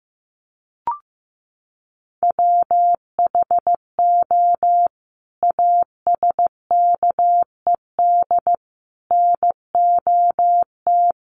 Morse code, heard as a keyed tone, about 15 wpm, 700 Hz. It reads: WHO ASKED NOT
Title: Sets of 5 Words Encore2 15wpm